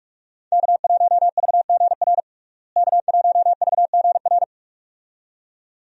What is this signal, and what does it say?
Morse code, heard as a keyed tone, about 46 wpm, 700 Hz. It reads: K1VGR K1VGR
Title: USA Call Signs - 5 Characters - Repeated 2x - 45wpm